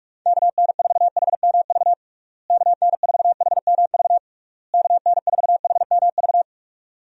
Morse code, heard as a keyed tone, about 46 wpm, 700 Hz. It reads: KN4HMV KN4HMV KN4HMV
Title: USA Call Signs - 6 Characters - Repeated 3x - 45wpm